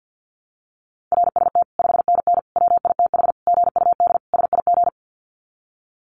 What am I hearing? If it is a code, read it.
Morse code, heard as a keyed tone, about 49 wpm, 700 Hz. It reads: RST 5NN WITH GUD SIG
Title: Common QSO Phrases 50wpm